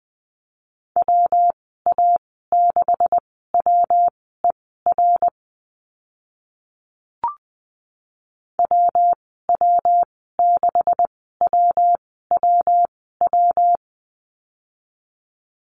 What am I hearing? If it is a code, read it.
Morse code, heard as a keyed tone, about 20 wpm, spaced as for 14 wpm, 700 Hz. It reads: WA6WER WW6WWW
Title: 140 Lesson 23g - Letter f - Call Signs 20wpm